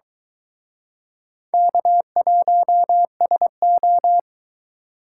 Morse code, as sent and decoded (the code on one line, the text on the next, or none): -.- .---- ... ---
K1SO